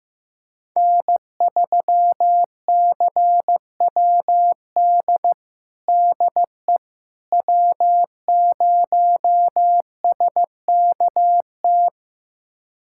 Morse code, heard as a keyed tone, about 15 wpm, 700 Hz. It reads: N3CWD DE W0SKT
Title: USA Callsign DE Callsign - 5 Characters - 15wpm